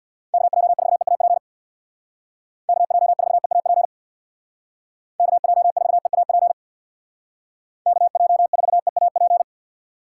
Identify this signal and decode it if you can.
Morse code, heard as a keyed tone, about 60 wpm, 700 Hz. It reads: KJ4EAP KJ4EAP KJ4EAP KJ4EAP
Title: Speed Racing - USA 2x3 Call Signs 50wpm